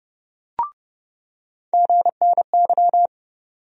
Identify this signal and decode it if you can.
Morse code, heard as a keyed tone, about 30 wpm, 700 Hz. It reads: GNY
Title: USA Call Sign Postfixes - 3 Letters 30wpm